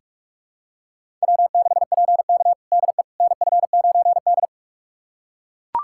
Morse code, heard as a keyed tone, about 42 wpm, 700 Hz. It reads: W6PK DE NF9D
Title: USA Callsign DE Callsign - 4 Characters - 45wpm